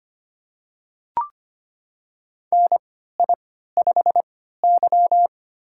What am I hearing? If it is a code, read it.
Morse code, heard as a keyed tone, about 25 wpm, spaced as for 14 wpm, 700 Hz. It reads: NI5Y
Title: USA Call Signs - 4 Characters - 3x Character Spacing - 25wpm